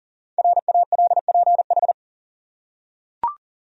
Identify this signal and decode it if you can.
Morse code, heard as a keyed tone, about 40 wpm, 700 Hz. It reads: RALPH
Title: USA Common Names - Max 5 Letters - 40wpm